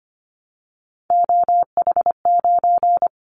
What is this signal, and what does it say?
Morse code, heard as a keyed tone, about 25 wpm, 700 Hz. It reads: OH9
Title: Callsign Prefixes for Top 25 DXCC - 25wpm